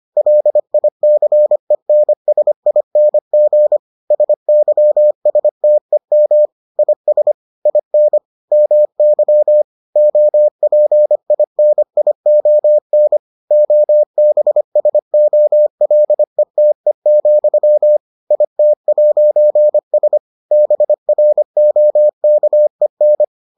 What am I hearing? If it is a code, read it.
Morse code, heard as a keyed tone, about 25 wpm, 600 Hz. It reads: LICENSING SYSTEM IS IN MY OPINION OBSOLETE, IT'S BROKEN